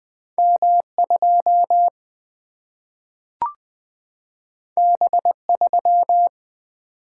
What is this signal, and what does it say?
Morse code, heard as a keyed tone, about 20 wpm, 700 Hz. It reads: M2 B3
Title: Single-Letter Single-Number 20wpm